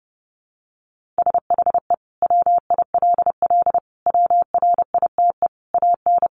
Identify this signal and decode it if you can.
Morse code, heard as a keyed tone, about 30 wpm, 700 Hz. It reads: SHE WILL WRITE AN
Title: Sentences from Top 1500 Words 30wpm